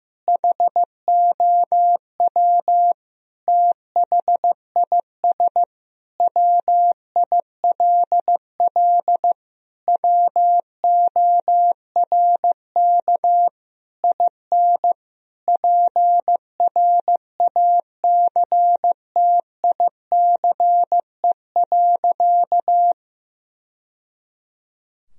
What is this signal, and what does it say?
Morse code, heard as a keyed tone, about 16 wpm, 700 Hz. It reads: HOW THIS WILL WORK IN PRACTICE.
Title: Sentences from Top 1000 Words - Review Part 4 - 15wpm